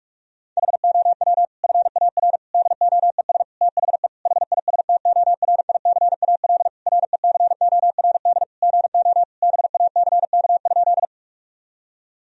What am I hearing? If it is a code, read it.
Morse code, heard as a keyed tone, about 41 wpm, 700 Hz. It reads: HOW FAR DOES THE HISTORICAL RECORD GO BACK?
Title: Sentences from Top 1000 Words - Review Part 4 - 45wpm